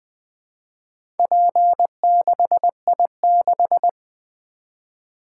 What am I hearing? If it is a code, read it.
Morse code, heard as a keyed tone, about 20 wpm, 700 Hz. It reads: P6I6
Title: Four Letter-Number Combinations - 20wpm